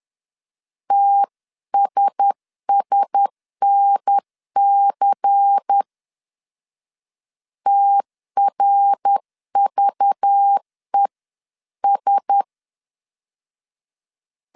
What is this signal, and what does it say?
Morse code, heard as a keyed tone, about 10 wpm, 800 Hz. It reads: TSSNC TRVES
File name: CW / DET / 1 6 1500C S C 10wpm 4252